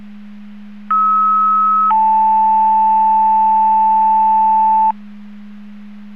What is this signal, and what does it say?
Two-tone page, heard as a one-second tone, then a three-second tone, about 1290 then 880 Hz. It is heard from 0.9 to 4.9 s.